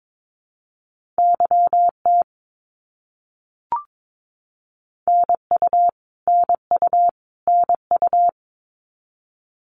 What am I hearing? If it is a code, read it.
Morse code, heard as a keyed tone, about 22 wpm, 700 Hz. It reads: YT NU NU NU
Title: Canadian Provinces and Territories - Repeated 3x - 22wpm